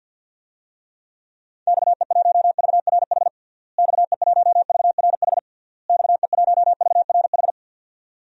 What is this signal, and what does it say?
Morse code, heard as a keyed tone, about 49 wpm, 700 Hz. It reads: XE1VRH XE1VRH XE1VRH
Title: DXCC-100 Call Signs - 6 Characters - Repeated 3x - 50wpm